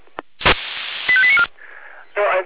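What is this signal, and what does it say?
Five-tone selective call (ZVEI), heard as quick runs of five tones, tones of about 70 ms. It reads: 85894
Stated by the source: Five-tone ZVEI selective call received on an SDR.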